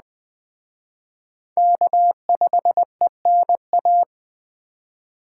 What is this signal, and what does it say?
Morse code, heard as a keyed tone, about 19 wpm, 700 Hz. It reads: K5ENA